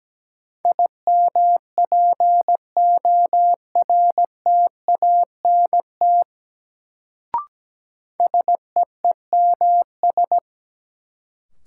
Morse code, heard as a keyed tone, about 17 wpm, 700 Hz. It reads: IMPORTANT SEEMS
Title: Sets of 3 Words - New Word Drill - 17wpm